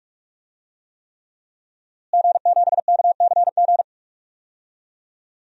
Morse code, heard as a keyed tone, about 46 wpm, 700 Hz. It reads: G7KCG
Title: DXCC-100 Call Signs - 5 Characters - Repeated 3x - 45wpm